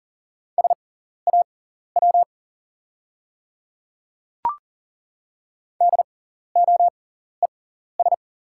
Morse code, heard as a keyed tone, about 40 wpm, spaced as for 14 wpm, 690 Hz. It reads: SAW DOES
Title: 6x Character Spacing - Top 300 Words - 40wpm